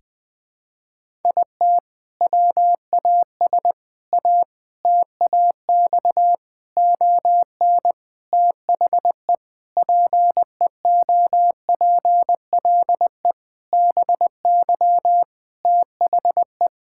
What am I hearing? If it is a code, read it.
Morse code, heard as a keyed tone, about 20 wpm, 700 Hz. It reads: IT WAS A TAX ON THE PEOPLE BY THE
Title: Sentences from Top 1000 Words - Review Part 5 - 20wpm